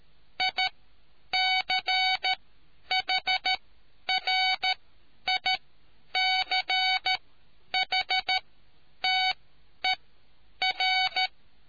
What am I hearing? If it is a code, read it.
Morse code, heard as a keyed tone, about 13 wpm, spaced as for 9 wpm, 750 Hz. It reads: ICHRICHTER